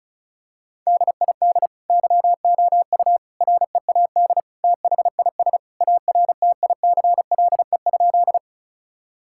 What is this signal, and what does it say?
Morse code, heard as a keyed tone, about 36 wpm, 700 Hz. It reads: DID YOU READ THIS ARTICLE?
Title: Sentences from Top 1000 Words - Review Part 5 - 35wpm